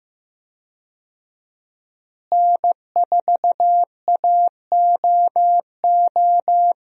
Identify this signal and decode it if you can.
Morse code, heard as a keyed tone, about 15 wpm, 700 Hz. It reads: N4AOO
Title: USA Call Signs - 5 Characters - Repeated 2x - 15wpm